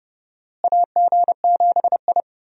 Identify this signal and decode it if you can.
Morse code, heard as a keyed tone, about 32 wpm, 700 Hz. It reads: AG7I
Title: USA Call Signs - 4 Characters - Repeated 3x - 30wpm